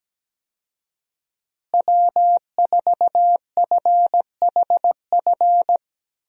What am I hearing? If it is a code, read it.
Morse code, heard as a keyed tone, about 17 wpm, 700 Hz. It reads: W4FHF
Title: USA Call Signs - 5 Characters - Repeated 3x - 17wpm